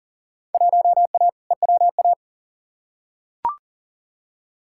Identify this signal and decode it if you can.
Morse code, heard as a keyed tone, about 40 wpm, 700 Hz. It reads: 1A EWA
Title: Field Day Exchanges Encore - 40wpm